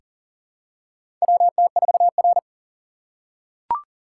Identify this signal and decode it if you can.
Morse code, heard as a keyed tone, about 40 wpm, 700 Hz. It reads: WT4R
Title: USA Call Signs - 4 Characters - 4x Character Spacing - 40wpm